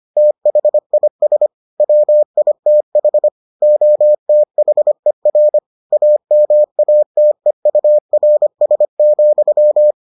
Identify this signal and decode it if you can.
Morse code, heard as a keyed tone, about 25 wpm, 600 Hz. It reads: THIS WITH OTHER AMATEURS,